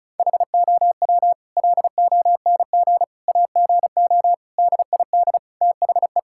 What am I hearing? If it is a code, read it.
Morse code, heard as a keyed tone, about 34 wpm, 700 Hz. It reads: HOW LONG AGO DID THE
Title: Sentences from Top 1000 Words - Review Part 6 - 35wpm